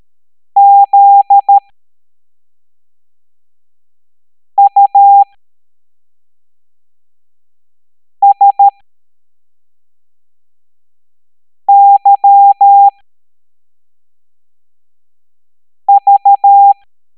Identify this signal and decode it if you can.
Morse code, heard as a keyed tone, about 13 wpm, spaced as for 3 wpm, 800 Hz. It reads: ZUSYV
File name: hamdata / L05-EX01-SVY